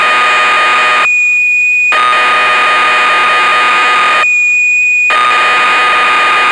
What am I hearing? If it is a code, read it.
Cassette data (Kansas City type).
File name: magazines / Acorn User / 1985-06 BBC Programs